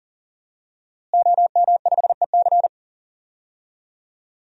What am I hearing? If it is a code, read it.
Morse code, heard as a keyed tone, about 40 wpm, 700 Hz. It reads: OM5EC
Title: DXCC-100 Call Signs - 5 Characters - Repeated 3x - 40wpm